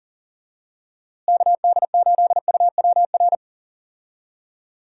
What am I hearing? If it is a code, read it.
Morse code, heard as a keyed tone, about 40 wpm, 700 Hz. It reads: KD8UWR
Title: USA Call Signs - 6 Characters - Repeated 2x - 40wpm